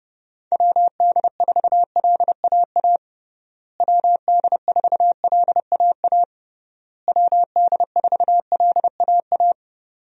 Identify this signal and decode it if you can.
Morse code, heard as a keyed tone, about 30 wpm, 700 Hz. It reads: WD4LAA WD4LAA WD4LAA
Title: USA Call Signs - 6 Characters - Repeated 3x - 30wpm